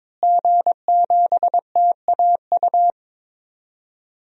Morse code, heard as a keyed tone, about 21 wpm, 700 Hz. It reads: G7TAU
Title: DXCC-100 Call Signs - 5 Characters - Repeated 2x - 22wpm